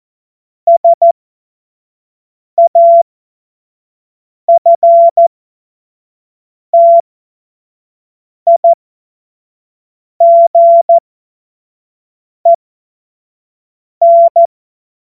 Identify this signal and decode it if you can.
Morse code, heard as a keyed tone, about 14 wpm, spaced as for 5 wpm, 680 Hz. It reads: SAFTIGEN
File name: Witz-23-September-2025-25-F70